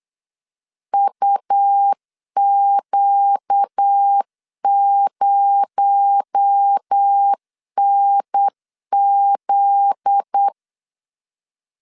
Morse code, heard as a keyed tone, about 8 wpm, 800 Hz. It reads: UQ0NZ
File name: LEZIONI / 8 YJ45 / 8 YJ4 8wpm 3246